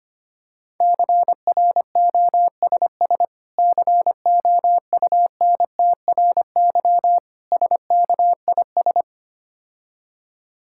Morse code, heard as a keyed tone, about 26 wpm, 700 Hz. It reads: CROSS COUNTRY SKIS
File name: practice / 25wpm / Difficult Words - Speed-Racing with Hints - 25wpm